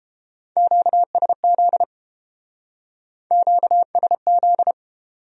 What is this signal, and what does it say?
Morse code, heard as a keyed tone, about 28 wpm, 700 Hz. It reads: QSZ QSZ